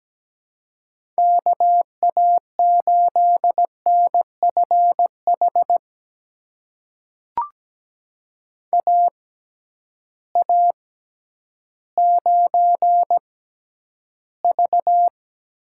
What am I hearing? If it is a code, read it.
Morse code, heard as a keyed tone, about 17 wpm, spaced as for 10 wpm, 700 Hz. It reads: KA8NFH AA9V
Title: USA Call Signs - 6x Character Spacing - 17wpm